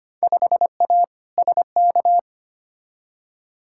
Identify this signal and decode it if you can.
Morse code, heard as a keyed tone, about 25 wpm, 700 Hz. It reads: 5A SK
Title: Field Day Exchanges Encore - 25wpm